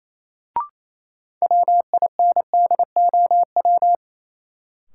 Morse code, heard as a keyed tone, about 28 wpm, 700 Hz. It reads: WINDOW
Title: Sentences from Top 1600 Words - New Word Drill - 28wpm